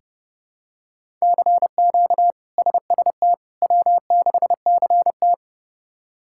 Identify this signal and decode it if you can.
Morse code, heard as a keyed tone, about 30 wpm, 700 Hz. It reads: CQ SST W6CT
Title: K1USN SST QSO Practice - 30wpm